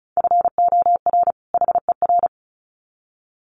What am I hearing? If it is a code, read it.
Morse code, heard as a keyed tone, about 34 wpm, 700 Hz. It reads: FOR HER
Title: Sets of 2 Words 35wpm